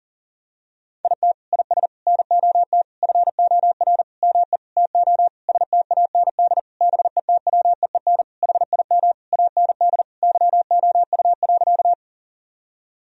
Morse code, heard as a keyed tone, about 40 wpm, 700 Hz. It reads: IT IS NOT FOR ME TO STAND BETWEEN HIM AND YOU.
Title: Sentences from Top 1000 Words - Review Part 1 - 40wpm